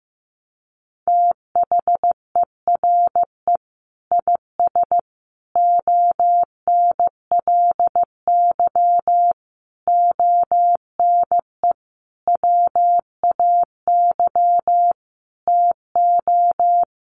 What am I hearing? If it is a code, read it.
Morse code, heard as a keyed tone, about 15 wpm, 700 Hz. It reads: THERE IS ONLY ONE WAY TO
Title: Sentences from Top 1200 Words 15wpm